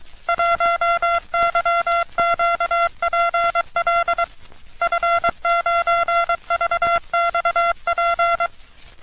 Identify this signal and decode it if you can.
Morse code, heard as a keyed tone, about 23 wpm, 1370 Hz. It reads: 1YQPL F9VXP
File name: U.S. Navy - RN-2849-124 - Sixteens Morse Code Practice